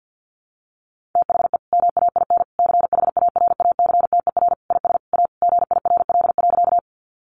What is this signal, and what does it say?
Morse code, heard as a keyed tone, about 50 wpm, 700 Hz. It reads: THE MAIN CHARACTER IS A GIRL.
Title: Additional Easy Phrases 50wpm